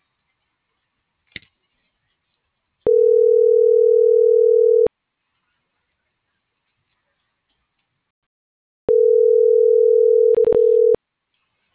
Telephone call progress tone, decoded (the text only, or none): ringback tone